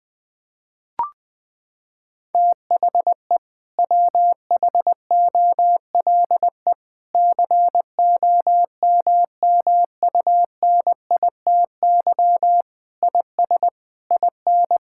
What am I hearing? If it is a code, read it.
Morse code, heard as a keyed tone, about 20 wpm, 700 Hz. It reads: THE WHOLE COMMUNITY IS IN
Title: Sentences from Top 1600 Words 20wpm